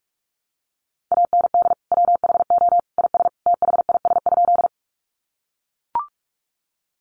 Morse code, heard as a keyed tone, about 44 wpm, 700 Hz. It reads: AND WHO IS THIS?
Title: Sentences from Top 100 Words 45wpm